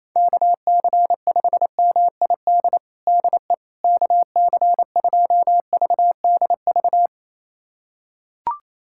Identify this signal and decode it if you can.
Morse code, heard as a keyed tone, about 27 wpm, 700 Hz. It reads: KC5MID DE KC2VDV
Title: USA Callsign DE Callsign - 6 Characters - 28wpm